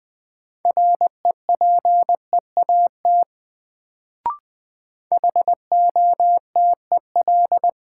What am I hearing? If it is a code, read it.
Morse code, heard as a keyed tone, about 21 wpm, 700 Hz. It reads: REPEAT HOTEL
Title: Sentences from Top 1600 Words - New Word Drill - 20wpm